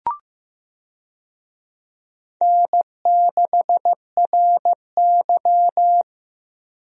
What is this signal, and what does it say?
Morse code, heard as a keyed tone, about 16 wpm, 700 Hz. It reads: N6RY
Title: USA Call Signs - 4 Characters - Repeated 3x - 15wpm